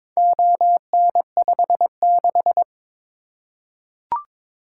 Morse code, heard as a keyed tone, about 22 wpm, 700 Hz. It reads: ON56
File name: practice / 22wpm / Four Letter-Number Combinations - 22wpm